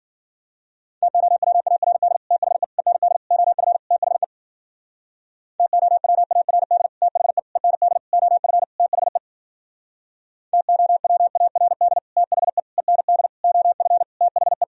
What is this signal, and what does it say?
Morse code, heard as a keyed tone, about 51 wpm, 700 Hz. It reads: TOWARD THE END OF THE TOWARD THE END OF THE TOWARD THE END OF THE